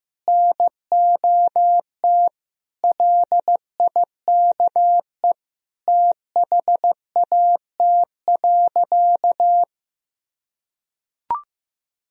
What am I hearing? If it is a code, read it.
Morse code, heard as a keyed tone, about 15 wpm, 700 Hz. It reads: NOT LIKE THAT.
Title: Sentences from Top 1000 Words - Review Part 3 - 15wpm